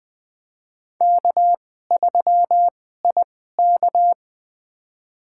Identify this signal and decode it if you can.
Morse code, heard as a keyed tone, about 20 wpm, spaced as for 14 wpm, 700 Hz. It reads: K3IK
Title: USA Call Signs - 4 Characters - 2x Character Spacing - 20wpm